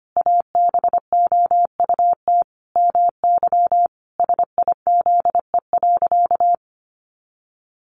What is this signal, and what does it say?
Morse code, heard as a keyed tone, about 25 wpm, 700 Hz. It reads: ABOUT MY SIZE.
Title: Sentences from Top 400 Words 25wpm